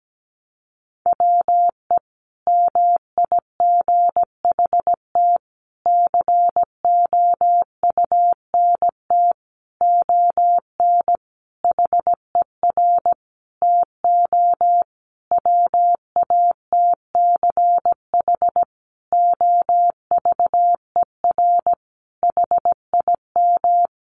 Morse code, heard as a keyed tone, about 17 wpm, 700 Hz. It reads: WE MIGHT COUNT ON HER TO WATCH OVER HIM
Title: Sentences from Top 1500 Words 17wpm